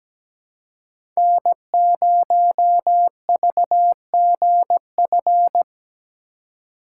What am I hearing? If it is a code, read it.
Morse code, heard as a keyed tone, about 17 wpm, 700 Hz. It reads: N0VGF
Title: USA Call Signs - 5 Characters - 4x Character Spacing - 17wpm